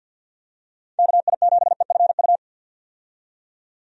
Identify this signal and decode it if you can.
Morse code, heard as a keyed tone, about 52 wpm, 700 Hz. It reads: KI7EFU
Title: USA Call Signs - 6 Characters - 50wpm